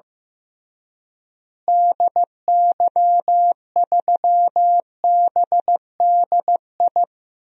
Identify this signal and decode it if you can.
Morse code, heard as a keyed tone, about 16 wpm, 700 Hz. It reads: DY3BDI
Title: DXCC-100 Call Signs - 6 Characters - Repeated 3x - 15wpm